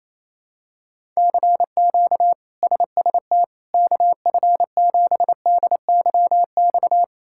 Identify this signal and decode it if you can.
Morse code, heard as a keyed tone, about 28 wpm, 700 Hz. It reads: CQ SST KF7DYX
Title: K1USN SST QSO Practice - 28wpm